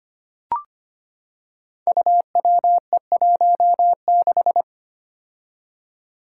Morse code, heard as a keyed tone, about 25 wpm, 700 Hz. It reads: UWE16
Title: Five Challenging Letter-Number Combinations 25wpm